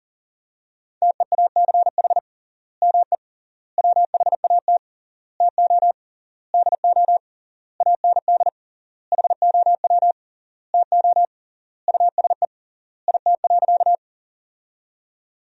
Morse code, heard as a keyed tone, about 40 wpm, 700 Hz. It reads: TEACH ME WHAT TO DO AND HOW TO USE IT.
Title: 3x Word-Spacing - Sentences from Top 500 Words - 40wpm